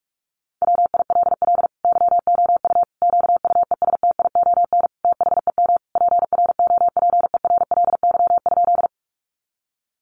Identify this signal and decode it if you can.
Morse code, heard as a keyed tone, about 44 wpm, 700 Hz. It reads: WILL YOU QUESTION THEM PROPERLY?
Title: Sentences from Top 1100 Words 45wpm